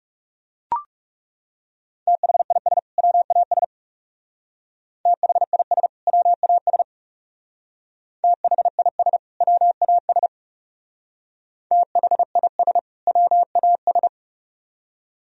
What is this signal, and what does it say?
Morse code, heard as a keyed tone, about 36 wpm, 700 Hz. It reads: THIS WAS THIS WAS THIS WAS THIS WAS
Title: Speed-Racing - Sets of 2 Words 30wpm